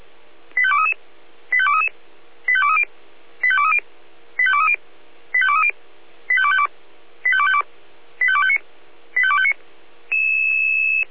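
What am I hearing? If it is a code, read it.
Five-tone selective call (ZVEI), heard as quick runs of five tones, tones of about 80 ms. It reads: 85233 85233 85230 85230 85230 85230 85252 85252 85259 85259
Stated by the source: ZVEI five-tone call received with an SDR.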